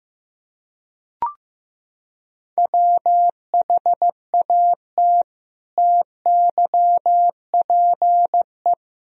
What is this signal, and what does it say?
Morse code, heard as a keyed tone, about 15 wpm, 700 Hz. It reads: WHAT TYPE
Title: Sentences from Top 1000 Words - Review Part 3 - 15wpm